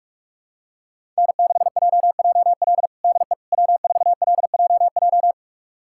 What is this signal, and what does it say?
Morse code, heard as a keyed tone, about 46 wpm, 700 Hz. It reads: N6JJL DE W4LJJ
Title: USA Callsign DE Callsign - 5 Characters - 45wpm